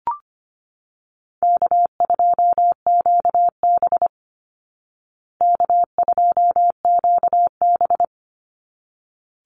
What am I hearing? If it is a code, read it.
Morse code, heard as a keyed tone, about 25 wpm, 700 Hz. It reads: K2QB K2QB
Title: Easy Call Signs Repeated 2x - 25wpm